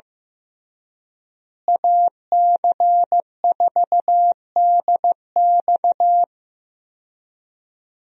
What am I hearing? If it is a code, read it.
Morse code, heard as a keyed tone, about 16 wpm, 700 Hz. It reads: AC4DX
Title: USA Call Signs - 5 Characters - Repeated 3x - 15wpm